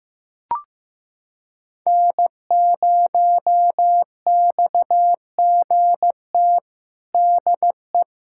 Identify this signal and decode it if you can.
Morse code, heard as a keyed tone, about 15 wpm, 700 Hz. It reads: N0XGT DE
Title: USA Callsign DE Callsign - 5 Characters - 15wpm